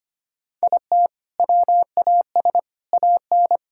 Morse code, heard as a keyed tone, about 25 wpm, 700 Hz. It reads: IT WAS AN
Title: Sentences from Top 1000 Words - Review Part 6 - 25wpm